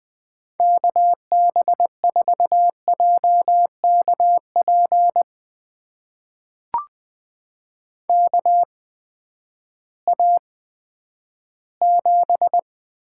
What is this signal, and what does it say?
Morse code, heard as a keyed tone, about 20 wpm, spaced as for 11 wpm, 700 Hz. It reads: KB4JKP KA7
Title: USA Call Signs - 6 Characters - 8x Character Spacing - 20wpm